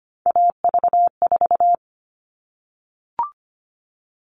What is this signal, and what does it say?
Morse code, heard as a keyed tone, about 25 wpm, 700 Hz. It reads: AV4
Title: All Cut Numbers - 3-Digit 25wpm